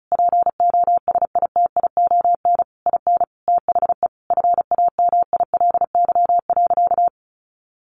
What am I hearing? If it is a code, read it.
Morse code, heard as a keyed tone, about 34 wpm, 700 Hz. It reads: POSITION IN THE FAMILY.
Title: Sentences from Top 1200 Words 35wpm